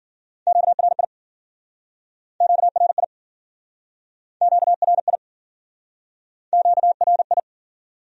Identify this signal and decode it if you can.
Morse code, heard as a keyed tone, about 50 wpm, 700 Hz. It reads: QRI QRI QRI QRI